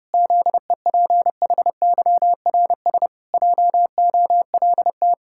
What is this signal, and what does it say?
Morse code, heard as a keyed tone, about 32 wpm, 700 Hz. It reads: ZEPHYRS JOLT
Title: Pangram Challenge v2 - 30wpm